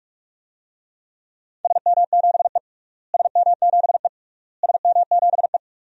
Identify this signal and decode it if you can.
Morse code, heard as a keyed tone, about 47 wpm, 700 Hz. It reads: SM7E SM7E SM7E
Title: DXCC-100 Call Signs - 4 Characters - Repeated 3x - 45wpm